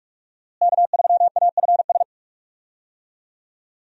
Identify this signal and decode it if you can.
Morse code, heard as a keyed tone, about 46 wpm, 700 Hz. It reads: K3AFS
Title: USA Call Signs - 5 Characters - Repeated 2x - 45wpm